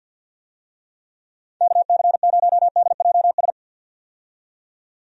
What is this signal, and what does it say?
Morse code, heard as a keyed tone, about 46 wpm, 700 Hz. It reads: KC0DJS
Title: USA Call Signs - 6 Characters - 50wpm